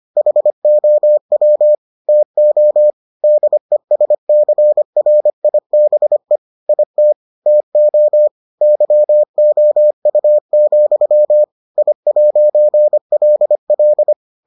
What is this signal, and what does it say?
Morse code, heard as a keyed tone, about 25 wpm, 600 Hz. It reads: HOW TO DESCRIBE IT TO YOU, I'LL